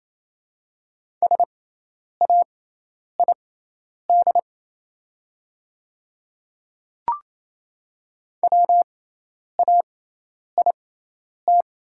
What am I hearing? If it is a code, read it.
Morse code, heard as a keyed tone, about 28 wpm, spaced as for 10 wpm, 700 Hz. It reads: SAID WAIT
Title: 6x Character Spacing - Top 500 Words - 28wpm